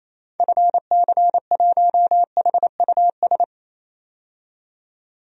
Morse code, heard as a keyed tone, about 28 wpm, 700 Hz. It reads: FC1HUS
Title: DXCC-100 Call Signs - 6 Characters - 28wpm